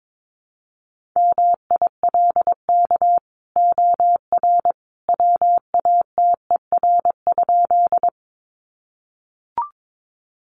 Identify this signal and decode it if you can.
Morse code, heard as a keyed tone, about 22 wpm, 700 Hz. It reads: MILK OR WATER?
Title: Sentences from Top 1500 Words 22wpm